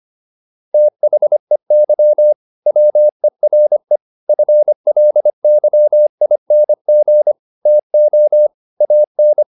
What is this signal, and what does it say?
Morse code, heard as a keyed tone, about 25 wpm, 600 Hz. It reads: THEY WERE FLYING TO AN